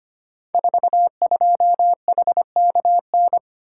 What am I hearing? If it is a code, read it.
Morse code, heard as a keyed tone, about 25 wpm, 700 Hz. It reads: 42HKN
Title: Code Groups - 5 Letter:Numbers - 25wpm